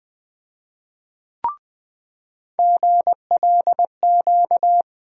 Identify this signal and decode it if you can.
Morse code, heard as a keyed tone, about 20 wpm, 700 Hz. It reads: GLQ
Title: USA Call Sign Postfixes - 3 Letters 20wpm